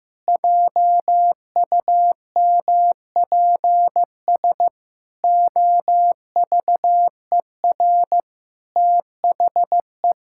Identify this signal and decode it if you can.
Morse code, heard as a keyed tone, about 15 wpm, 700 Hz. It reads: JUMPS OVER THE
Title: Pangram Challenge v2 - 15wpm